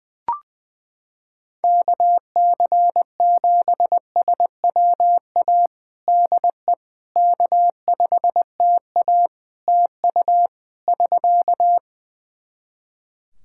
Morse code, heard as a keyed tone, about 20 wpm, 700 Hz. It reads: KC7SWA DE K5TA TU <SK>
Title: Practice QSO - Exchange 4 - Ending - 20wpm